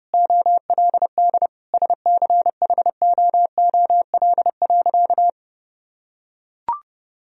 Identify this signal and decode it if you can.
Morse code, heard as a keyed tone, about 31 wpm, 700 Hz. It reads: OLD SCHOOL.
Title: Sentences from Top 1000 Words - Review Part 1 - 30wpm